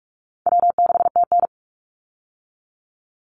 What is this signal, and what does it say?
Morse code, heard as a keyed tone, about 44 wpm, 700 Hz. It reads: W6TN
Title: Easy Call Signs - 45wpm